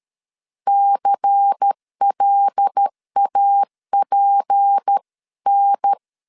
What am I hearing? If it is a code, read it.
Morse code, heard as a keyed tone, about 13 wpm, 800 Hz. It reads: CLAPN
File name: CW / DET / 1 6 1500C S A 12wpm 3440